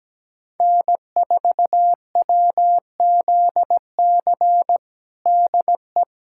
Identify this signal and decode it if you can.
Morse code, heard as a keyed tone, about 17 wpm, 700 Hz. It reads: N4WZC DE
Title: USA Callsign DE Callsign 17wpm